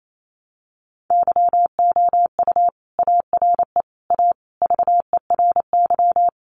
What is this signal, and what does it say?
Morse code, heard as a keyed tone, about 28 wpm, 700 Hz. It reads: YOU ARE A VERY
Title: Sentences from Top 1100 Words 28wpm